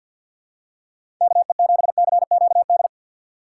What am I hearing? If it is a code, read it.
Morse code, heard as a keyed tone, about 52 wpm, 700 Hz. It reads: KE7CQD
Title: USA Call Signs - 6 Characters - 6x Character Spacing - 50wpm